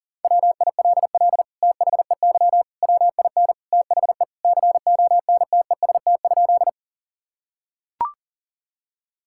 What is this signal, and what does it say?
Morse code, heard as a keyed tone, about 40 wpm, 700 Hz. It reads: WILL THEY WIN THE CONTEST?
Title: Sentences from Top 1600 Words 40wpm